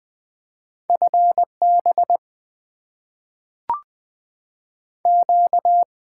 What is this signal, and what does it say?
Morse code, heard as a keyed tone, about 20 wpm, 700 Hz. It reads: FB Q
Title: Top 20 QSO Elements - 6x Character Spacing - 20wpm